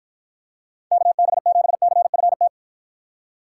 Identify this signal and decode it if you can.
Morse code, heard as a keyed tone, about 49 wpm, 700 Hz. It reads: KB7CFT